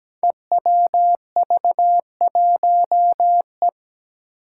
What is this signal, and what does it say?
Morse code, heard as a keyed tone, about 17 wpm, 700 Hz. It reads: EWV1E
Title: Five Challenging Letter-Number Combinations 17wpm